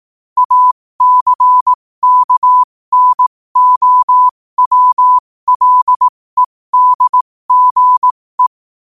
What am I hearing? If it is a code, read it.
Morse code, heard as a keyed tone, about 18 wpm, spaced as for 15 wpm, 1000 Hz. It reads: ACKNOWLEDGE
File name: qotdfiles / 15wpm